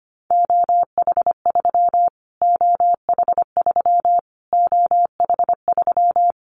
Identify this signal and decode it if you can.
Morse code, heard as a keyed tone, about 25 wpm, 700 Hz. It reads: OH3 OH3 OH3
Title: Callsign Prefixes for Top 25 DXCC - Repeated 3x - 25wpm